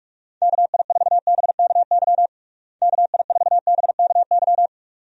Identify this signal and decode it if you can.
Morse code, heard as a keyed tone, about 46 wpm, 700 Hz. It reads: KI4BKY KI4BKY
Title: USA Call Signs - 6 Characters - Repeated 2x - 45wpm